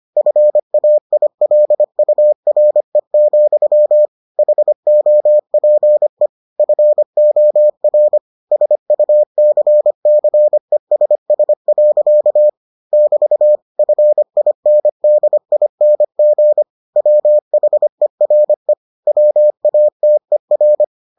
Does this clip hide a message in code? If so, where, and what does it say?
Yes: Morse code from 0.2 to 20.8 s, heard as a keyed tone, about 25 wpm, 600 Hz. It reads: FAILURE, HOPE FOR SUCCESS. <BT> FINDING WHERE WATER